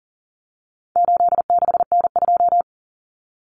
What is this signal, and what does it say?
Morse code, heard as a keyed tone, about 40 wpm, 700 Hz. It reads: 86N2
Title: Common Cut Numbers - 4-Digit 40wpm